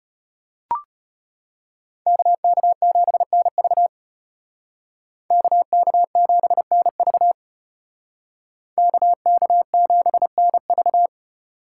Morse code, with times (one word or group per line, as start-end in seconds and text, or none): 2.1-3.9 KK7NV
5.3-7.3 KK7NV
8.8-11.1 KK7NV